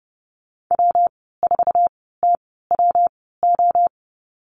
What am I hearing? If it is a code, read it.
Morse code, heard as a keyed tone, about 31 wpm, spaced as for 17 wpm, 700 Hz. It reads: W4TWO
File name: Lesson 19g - Letter w - Call Signs 30wpm